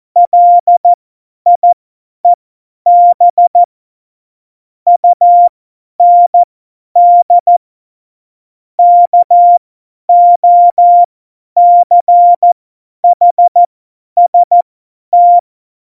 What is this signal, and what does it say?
Morse code, heard as a keyed tone, about 14 wpm, spaced as for 10 wpm, 700 Hz. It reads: LIEB UND KOCHST
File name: Morsekurs / Witze / Witz 20. August 2019, 50 F70